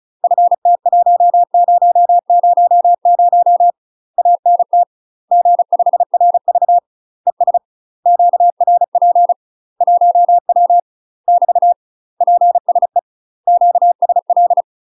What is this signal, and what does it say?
Morse code, heard as a keyed tone, about 34 wpm, 700 Hz. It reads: FT1000 ANT G5RV ES QRP 1W <BT> PSE QSL
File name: UK MASTER QSOs TRAINING/35WPM